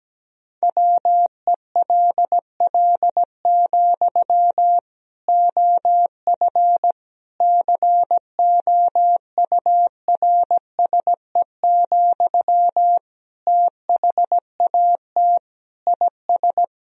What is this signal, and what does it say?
Morse code, heard as a keyed tone, about 17 wpm, 700 Hz. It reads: WELL, OF COURSE, THAT IS
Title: Sentences from Top 1000 Words - Review Part 1 - 17wpm